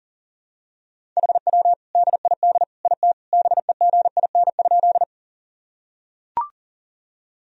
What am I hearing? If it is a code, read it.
Morse code, heard as a keyed tone, about 40 wpm, 700 Hz. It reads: HW DID IT BEGIN?
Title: Sentences from Top 300 Words and Abbreviations 40wpm